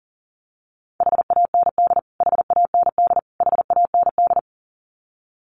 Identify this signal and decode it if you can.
Morse code, heard as a keyed tone, about 40 wpm, 700 Hz. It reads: HAND HAND HAND
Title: Top 500 Words Repeated 40wpm